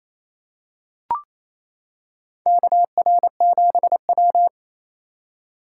Morse code, heard as a keyed tone, about 28 wpm, 700 Hz. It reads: KR7W
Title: K1USN SST QSO Practice - 28wpm